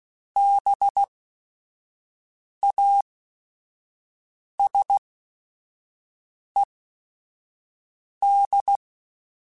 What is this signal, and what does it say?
Morse code, heard as a keyed tone, about 16 wpm, spaced as for 5 wpm, 800 Hz. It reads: BASED
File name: Technical Subjects / Technical Subjects-mp3s / Reactance @5